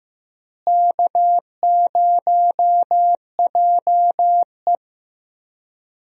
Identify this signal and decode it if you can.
Morse code, heard as a keyed tone, about 15 wpm, 700 Hz. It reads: K0JE
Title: USA Call Signs - 4 Characters - Repeated 2x - 15wpm